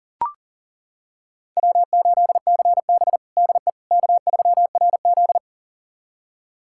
Morse code, heard as a keyed tone, about 37 wpm, 700 Hz. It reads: W8CB DE K3RZ
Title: USA Callsign DE Callsign - 4 Characters - 40wpm